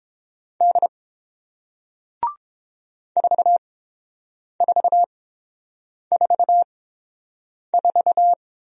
Morse code, heard as a keyed tone, about 30 wpm, spaced as for 7 wpm, 700 Hz. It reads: D 4444